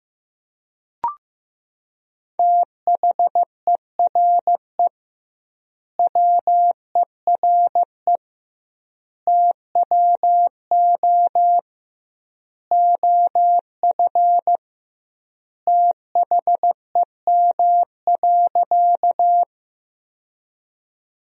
Morse code, heard as a keyed tone, about 16 wpm, 700 Hz. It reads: THERE WERE TWO OF THEM.
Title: 2x Word-Spacing - Sentences from Top 100 Words - 15wpm